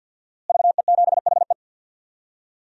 Morse code, heard as a keyed tone, about 48 wpm, 700 Hz. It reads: VE7HE
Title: DXCC-100 Call Signs - 5 Characters - Repeated 2x - 50wpm